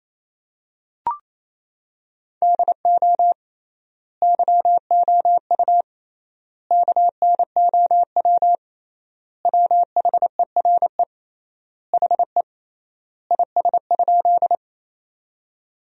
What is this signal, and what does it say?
Morse code, heard as a keyed tone, about 28 wpm, 700 Hz. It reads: DO YOU KNOW WHERE HE IS?
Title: 3x Word-Spacing - Sentences from Top 300 Words - 28wpm